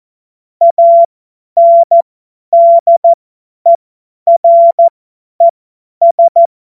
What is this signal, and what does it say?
Morse code, heard as a keyed tone, about 14 wpm, spaced as for 10 wpm, 680 Hz. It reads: ANDERES
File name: Morsekurs / Witze / Witz 10. Januar 2023 50 F70